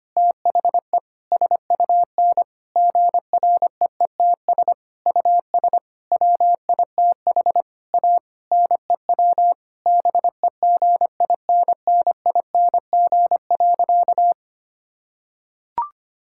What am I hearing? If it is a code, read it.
Morse code, heard as a keyed tone, about 25 wpm, 700 Hz. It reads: THE SUN GREETS US WITH A NEW BEGINNING.
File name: practice / 25wpm / Difficult Sentences with Hints - 25wpm